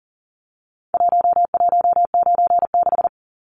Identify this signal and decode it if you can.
Morse code, heard as a keyed tone, about 40 wpm, 700 Hz. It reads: 1196
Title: Numbers - Contest Style - 4-Digit 40wpm